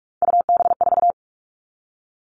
Morse code, heard as a keyed tone, about 46 wpm, 700 Hz. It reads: UB4
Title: Three Challenging Letter-Number Combinations 45wpm